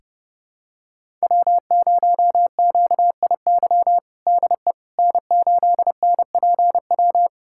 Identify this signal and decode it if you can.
Morse code, heard as a keyed tone, about 30 wpm, 700 Hz. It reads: W0QIY DE N8NPW
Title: USA Callsign DE Callsign - 5 Characters - 30wpm